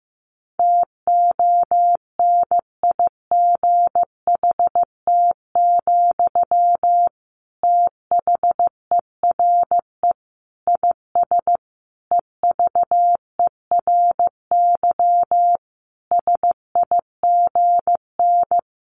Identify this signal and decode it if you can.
Morse code, heard as a keyed tone, about 15 wpm, 700 Hz. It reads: TONIGHT, THERE IS EVERY SIGN
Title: Sentences from Top 1400 Words 15wpm